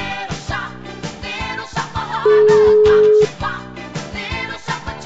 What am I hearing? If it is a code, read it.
Ringback tone (Europe, 425 Hz).